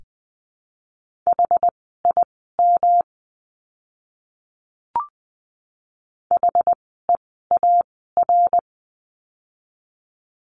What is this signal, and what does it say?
Morse code, heard as a keyed tone, about 20 wpm, spaced as for 14 wpm, 700 Hz. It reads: HIM HEAR
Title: 108 Lesson 18f - Letter m - Words 20wpm